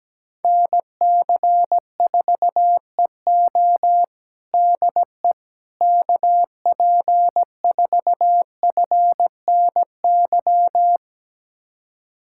Morse code, heard as a keyed tone, about 17 wpm, 700 Hz. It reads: NC4EO DE KP4FNY
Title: USA Callsign DE Callsign 17wpm